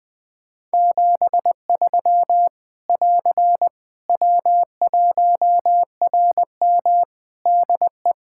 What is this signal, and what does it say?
Morse code, heard as a keyed tone, about 19 wpm, 700 Hz. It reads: 73 <AR> W1RM DE
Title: Practice QSO - Exchange 4 - Ending - 20wpm